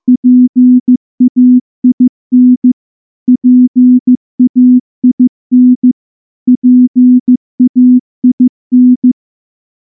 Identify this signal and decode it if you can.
Morse code, heard as a keyed tone, about 14 wpm, 250 Hz. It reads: PAIN PAIN PAIN
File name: morse / morse-training-15WPM